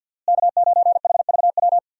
Morse code, heard as a keyed tone, about 51 wpm, 700 Hz. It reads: K9HVW
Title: USA Call Signs - 5 Characters - Repeated 2x - 50wpm